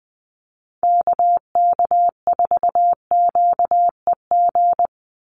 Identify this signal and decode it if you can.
Morse code, heard as a keyed tone, about 20 wpm, 700 Hz. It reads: KK4QEG
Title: USA Call Signs 20wpm